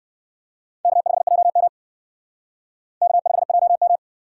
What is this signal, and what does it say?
Morse code, heard as a keyed tone, about 56 wpm, 700 Hz. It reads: K5JM K5JM
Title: Speed Racing - USA 1x2 Call Signs 45wpm